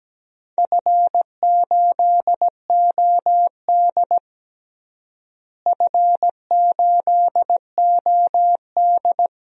Morse code, heard as a keyed tone, about 17 wpm, 700 Hz. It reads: F8OD F8OD
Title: DXCC-100 Call Signs - 4 Characters - Repeated 2x - 17wpm